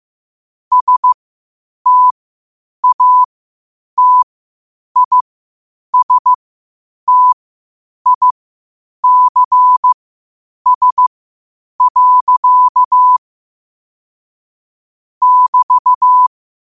Morse code, heard as a keyed tone, about 15 wpm, spaced as for 8 wpm, 1000 Hz. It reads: STATISTICS. <BT>
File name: qotdfiles / 8wpm